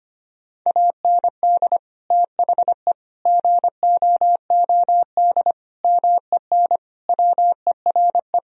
Morse code, heard as a keyed tone, about 25 wpm, 700 Hz. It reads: AND THE GOOD MEN WERE
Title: Sentences from Top 1000 Words - Review Part 2 - 25wpm